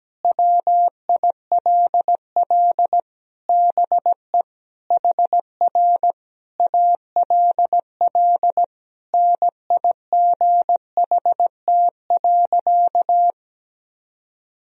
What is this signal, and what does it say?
Morse code, heard as a keyed tone, about 17 wpm, 700 Hz. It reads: WILL BE HR ALL NIGHT.
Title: Sentences from Top 300 Words and Abbreviations 17wpm